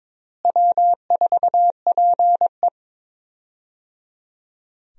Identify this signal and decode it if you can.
Morse code, heard as a keyed tone, about 22 wpm, 700 Hz. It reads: W4PE
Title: USA Call Signs - 4 Characters - Repeated 3x - 22wpm